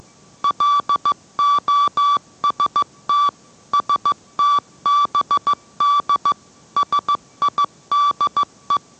Morse code, heard as a keyed tone, about 15 wpm, 1160 Hz. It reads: LOST STBD SIDE